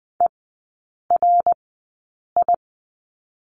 Morse code, heard as a keyed tone, about 20 wpm, spaced as for 8 wpm, 700 Hz. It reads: ERI
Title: 060 Lesson 11e - Letter h - Three Characters 20wpm